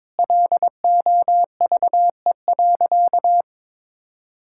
Morse code, heard as a keyed tone, about 22 wpm, 700 Hz. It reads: LOVE.
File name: practice / 22wpm / Sentences with 4-Word Phrases - 22wpm